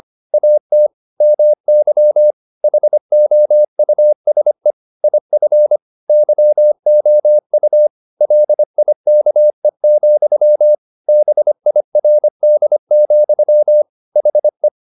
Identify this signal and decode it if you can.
Morse code, heard as a keyed tone, about 25 wpm, 600 Hz. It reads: AT MY HOUSE IF YOU LIKE, BIRD, HE
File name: the old man and the sea part 1 25wpm